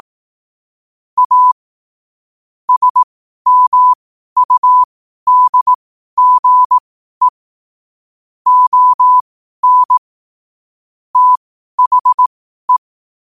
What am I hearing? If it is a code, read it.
Morse code, heard as a keyed tone, about 18 wpm, spaced as for 12 wpm, 1000 Hz. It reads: A SMUDGE ON THE
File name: qotdfiles / 12wpm